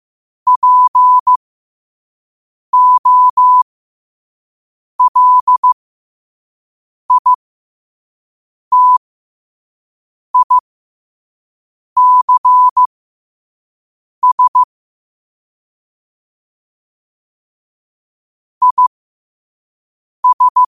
Morse code, heard as a keyed tone, about 15 wpm, spaced as for 5 wpm, 1000 Hz. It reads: POLITICS IS
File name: qotdfiles / 5wpm